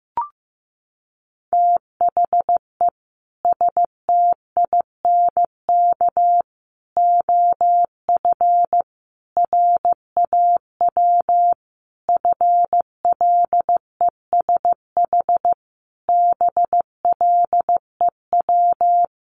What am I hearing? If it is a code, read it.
Morse code, heard as a keyed tone, about 15 wpm, 700 Hz. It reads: THE STINK OF RAW FLESH BLEW
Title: A Sound Of Thunder 15wpm